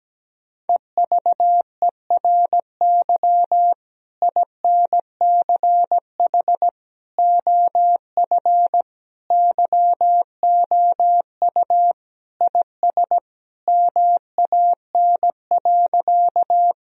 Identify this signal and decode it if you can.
Morse code, heard as a keyed tone, about 17 wpm, 700 Hz. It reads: EVERY INCH OF YOU IS MAN.
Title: Sentences from Top 1000 Words - Review Part 3 - 17wpm